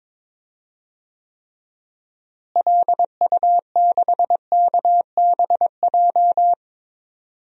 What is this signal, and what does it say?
Morse code, heard as a keyed tone, about 22 wpm, 700 Hz. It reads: LU6KBJ
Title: DXCC-100 Call Signs - 6 Characters - Repeated 3x - 22wpm